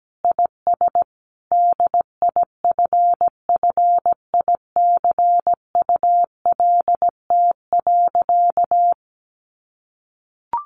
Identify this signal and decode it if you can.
Morse code, heard as a keyed tone, about 17 wpm, 700 Hz. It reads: IS DIFFICULT.
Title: Sentences from Top 1300 Words 17wpm